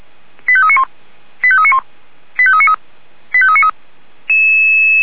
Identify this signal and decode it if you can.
Five-tone selective call (ZVEI), heard as quick runs of five tones, tones of about 70 ms. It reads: 85391 85391 85393 85393